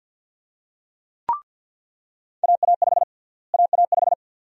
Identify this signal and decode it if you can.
Morse code, heard as a keyed tone, about 53 wpm, 700 Hz. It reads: AA5 AA5
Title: USA Call Sign Prefixes - Repeated 2x - 50wpm